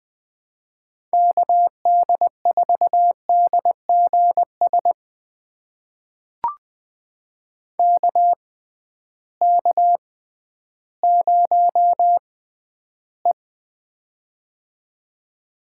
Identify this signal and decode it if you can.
Morse code, heard as a keyed tone, about 21 wpm, spaced as for 12 wpm, 700 Hz. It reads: KD4DGS KK0E
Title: USA Call Signs - 6x Character Spacing - 20wpm